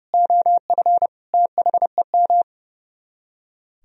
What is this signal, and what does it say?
Morse code, heard as a keyed tone, about 36 wpm, 700 Hz. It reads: OF THEM
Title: Speed-Racing - Sets of 3 Words 30wpm